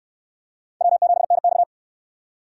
Morse code, heard as a keyed tone, about 64 wpm, 700 Hz. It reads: W7NX